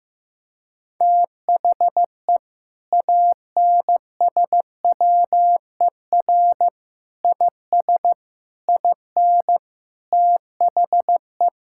Morse code, heard as a keyed tone, about 15 wpm, 700 Hz. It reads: THE ANSWER IS IN THE
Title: Sentences from Top 1000 Words - Review Part 5 - 15wpm